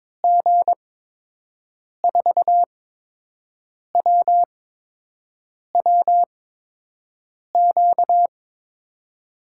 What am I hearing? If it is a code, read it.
Morse code, heard as a keyed tone, about 22 wpm, 700 Hz. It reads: G 4 W W Q
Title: USA Call Signs - 6 Characters - 8x Character Spacing - 22wpm